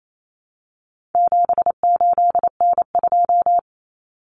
Morse code, heard as a keyed tone, about 28 wpm, 700 Hz. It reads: 78N2
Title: Common Cut Numbers - 4-Digit 28wpm